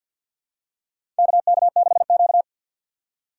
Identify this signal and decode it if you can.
Morse code, heard as a keyed tone, about 52 wpm, 700 Hz. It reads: KK6Q
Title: USA Call Signs - 4 Characters - 3x Character Spacing - 50wpm